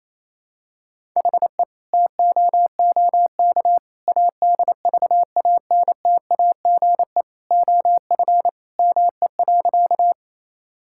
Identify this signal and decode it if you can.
Morse code, heard as a keyed tone, about 28 wpm, 700 Hz. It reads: HE TOOK ADVANTAGE OF ME.
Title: Sentences from Top 1000 Words - Review Part 6 - 28wpm